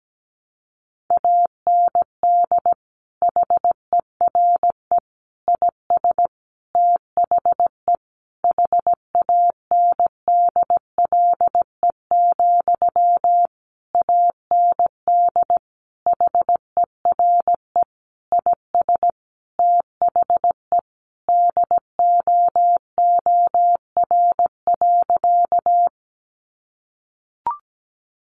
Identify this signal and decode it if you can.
Morse code, heard as a keyed tone, about 17 wpm, 700 Hz. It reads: AND HERE IS THE HANDLE, AND HERE IS THE DOOR.
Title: Sentences from Top 1100 Words 17wpm